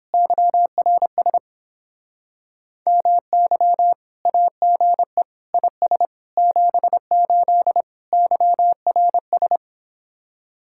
Morse code, heard as a keyed tone, about 29 wpm, 700 Hz. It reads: YRS MY AGE IS 78 YRS